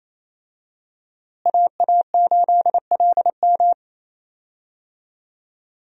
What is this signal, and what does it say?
Morse code, heard as a keyed tone, about 28 wpm, 700 Hz. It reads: AA8LM
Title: USA Call Signs - 5 Characters - Repeated 2x - 28wpm